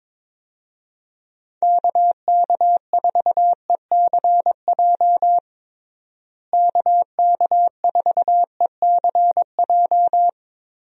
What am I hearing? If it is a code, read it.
Morse code, heard as a keyed tone, about 21 wpm, 700 Hz. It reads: KK4ECJ KK4ECJ
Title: USA Call Signs - 6 Characters - Repeated 3x - 22wpm